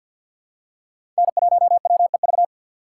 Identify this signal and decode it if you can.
Morse code, heard as a keyed tone, about 52 wpm, 700 Hz. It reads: N1WEV
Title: USA Call Signs - 5 Characters - 2x Character Spacing - 50wpm